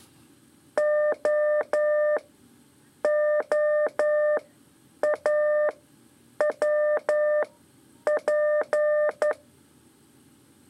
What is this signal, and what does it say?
Morse code, heard as a keyed tone, about 11 wpm, spaced as for 8 wpm, 600 Hz. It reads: OOAWP
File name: 3 lezione CW 2 AIR